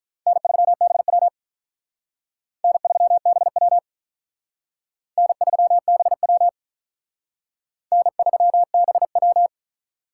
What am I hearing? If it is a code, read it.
Morse code, heard as a keyed tone, about 46 wpm, 700 Hz. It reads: N3BW N3BW N3BW N3BW